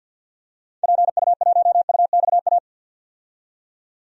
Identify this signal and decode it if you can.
Morse code, heard as a keyed tone, about 52 wpm, 700 Hz. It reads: PU1UXA
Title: DXCC-100 Call Signs - 6 Characters - 50wpm